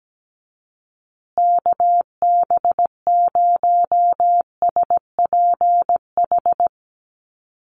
Morse code, heard as a keyed tone, about 17 wpm, 700 Hz. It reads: KB0SPH
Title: USA Call Signs Repeated 2x 17wpm